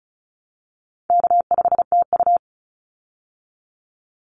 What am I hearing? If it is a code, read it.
Morse code, heard as a keyed tone, about 34 wpm, 700 Hz. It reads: K5TU
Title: Easy Call Signs Repeated 2x - 35wpm